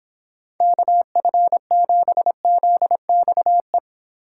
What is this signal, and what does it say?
Morse code, heard as a keyed tone, about 26 wpm, 700 Hz. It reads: KF7ZXE